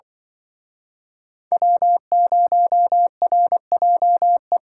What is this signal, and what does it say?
Morse code, heard as a keyed tone, about 23 wpm, 700 Hz. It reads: W0RJE
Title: Speed Racing - USA Call Signs 20wpm